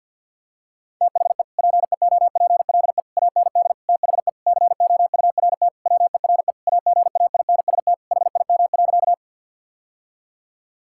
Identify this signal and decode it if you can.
Morse code, heard as a keyed tone, about 50 wpm, 700 Hz. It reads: THE PEOPLE AND THE COURT WERE AGAINST HIM.
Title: Sentences from Top 1600 Words 50wpm